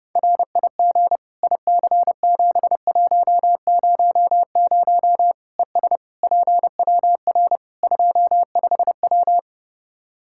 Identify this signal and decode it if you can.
Morse code, heard as a keyed tone, about 32 wpm, 700 Hz. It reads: RIG IC7100 ES PWR 25W
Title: K1USN SST QSO Practice 30wpm